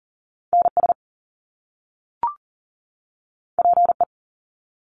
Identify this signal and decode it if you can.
Morse code, heard as a keyed tone, about 40 wpm, 700 Hz. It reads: NS PE
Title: Canadian Provinces and Territories 40wpm